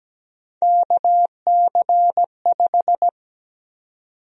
Morse code, heard as a keyed tone, about 17 wpm, 700 Hz. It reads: KC5
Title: USA Call Sign Prefixes - 17wpm